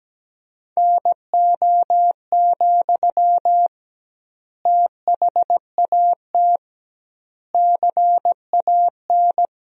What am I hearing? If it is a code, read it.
Morse code, heard as a keyed tone, about 17 wpm, 700 Hz. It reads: NO, THAT CAN
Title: 2x Word-Spacing - Sentences from Top 300 Words - 17wpm